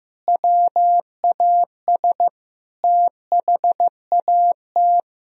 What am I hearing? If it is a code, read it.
Morse code, heard as a keyed tone, about 15 wpm, 700 Hz. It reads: WAS THAT
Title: Sentences from Top 1000 Words - Review Part 1 - 15wpm